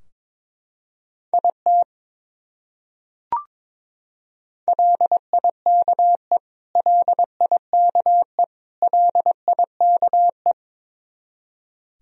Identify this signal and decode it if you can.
Morse code, heard as a keyed tone, about 22 wpm, 700 Hz. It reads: IT LIKE LIKE LIKE
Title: Top 100 Words Repeated 22wpm